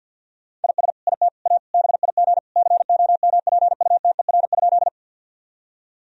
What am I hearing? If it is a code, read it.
Morse code, heard as a keyed tone, about 50 wpm, 700 Hz. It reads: IS IT A BIG COMPUTER?
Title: Sentences from Top 1000 Words - Review Part 3 - 50wpm